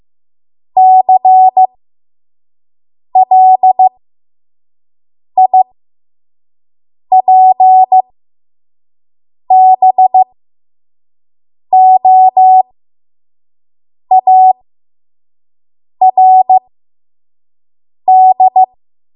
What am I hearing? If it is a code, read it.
Morse code, heard as a keyed tone, about 15 wpm, spaced as for 5 wpm, 750 Hz. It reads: CLIPBOARD
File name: Morse / 05WPM